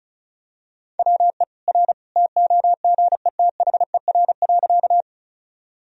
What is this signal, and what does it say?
Morse code, heard as a keyed tone, about 36 wpm, 700 Hz. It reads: WE R TOGETHER.
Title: Sentences from Top 300 Words and Abbreviations 35wpm